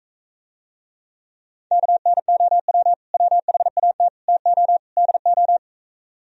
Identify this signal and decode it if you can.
Morse code, heard as a keyed tone, about 45 wpm, 700 Hz. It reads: KNOW WHAT TO DO